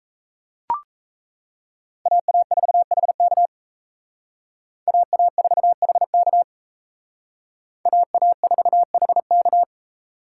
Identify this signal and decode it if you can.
Morse code, heard as a keyed tone, about 32 wpm, 700 Hz. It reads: AA4HK AA4HK AA4HK
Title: Speed Racing - USA Call Signs 28wpm